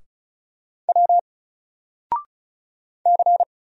Morse code, heard as a keyed tone, about 36 wpm, 700 Hz. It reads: W C
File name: Single-Letter-Number-35wpm